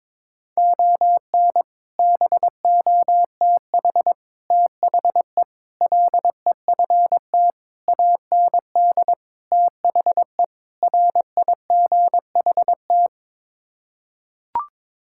Morse code, heard as a keyed tone, about 22 wpm, 700 Hz. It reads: ON BOTH THE LEFT AND THE RIGHT
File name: practice / 22wpm / Sets of 7 Words - 22wpm